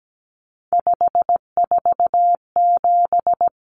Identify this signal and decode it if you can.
Morse code, heard as a keyed tone, about 17 wpm, 700 Hz. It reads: 547
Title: Numbers - 3-Digit 17wpm